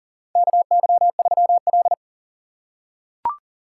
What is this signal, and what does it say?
Morse code, heard as a keyed tone, about 42 wpm, 700 Hz. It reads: KY3L
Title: USA Call Signs - 4 Characters - 4x Character Spacing - 40wpm